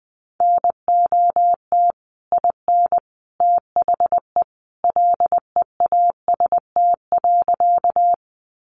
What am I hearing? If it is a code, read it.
Morse code, heard as a keyed tone, about 20 wpm, 700 Hz. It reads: NOT IN THE LEAST.
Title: Sentences from Top 700 Words 20wpm